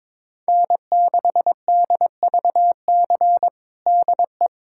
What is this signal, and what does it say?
Morse code, heard as a keyed tone, about 22 wpm, 700 Hz. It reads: N6DVC DE
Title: USA Callsign DE Callsign 22wpm